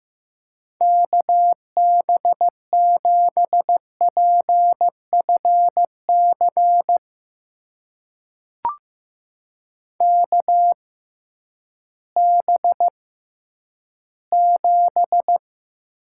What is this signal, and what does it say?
Morse code, heard as a keyed tone, about 15 wpm, spaced as for 10 wpm, 700 Hz. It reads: KB7PFC KB7
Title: USA Call Signs - 6 Characters - 6x Character Spacing - 15wpm